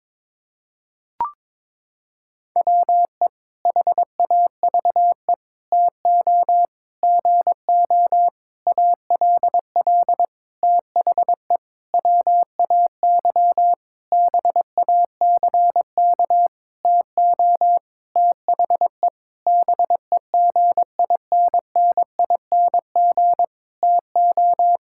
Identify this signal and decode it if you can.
Morse code, heard as a keyed tone, about 22 wpm, 700 Hz. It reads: WE HAVE TO GO ALL THE WAY BACK TO THE BEGINNING TO
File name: practice / 22wpm / Sentences with 8-Word Phrases - 22wpm